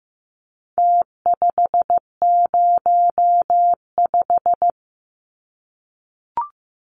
Morse code, heard as a keyed tone, about 15 wpm, 700 Hz. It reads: T505
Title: Numbers - Contest Style - 4-Digit 15wpm